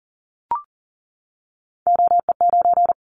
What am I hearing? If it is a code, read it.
Morse code, heard as a keyed tone, about 42 wpm, 700 Hz. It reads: OE9
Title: Callsign Prefixes for Top 50 DXCC - 40wpm